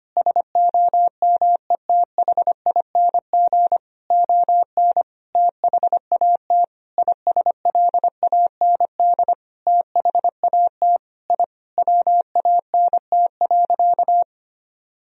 Morse code, heard as a keyed tone, about 25 wpm, 700 Hz. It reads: SOMETHING ON THAT ISLAND THAT I WANT.
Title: Sentences from Top 1000 Words - Review Part 3 - 25wpm